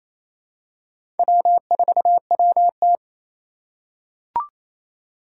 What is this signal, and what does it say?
Morse code, heard as a keyed tone, about 28 wpm, 700 Hz. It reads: W4WT
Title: USA Call Signs - 4x Character Spacing - 28wpm